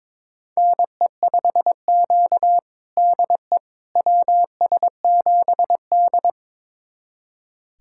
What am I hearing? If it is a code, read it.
Morse code, heard as a keyed tone, about 22 wpm, 700 Hz. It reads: NE5Q DE WS7D
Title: USA Callsign DE Callsign - 4 Characters - 22wpm